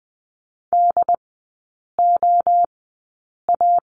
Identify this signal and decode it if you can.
Morse code, heard as a keyed tone, about 20 wpm, spaced as for 8 wpm, 700 Hz. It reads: DOA
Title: 074 Lesson 13e - Letter L - Three Characters 20wpm